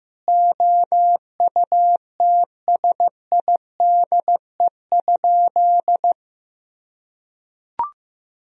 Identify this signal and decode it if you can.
Morse code, heard as a keyed tone, about 15 wpm, 700 Hz. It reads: OUTSIDE?
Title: Sentences from Top 1000 Words - Review Part 4 - 15wpm